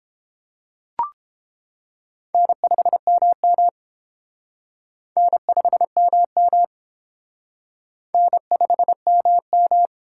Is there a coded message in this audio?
Yes: Morse code from 2.3 to 9.9 s, heard as a keyed tone, about 30 wpm, 700 Hz. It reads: N5MM N5MM N5MM